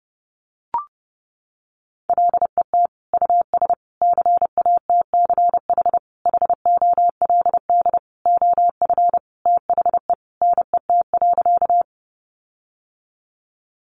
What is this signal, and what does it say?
Morse code, heard as a keyed tone, about 30 wpm, 700 Hz. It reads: LET US CATCH HOLD OF THE NET.
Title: Sentences from Top 1500 Words 30wpm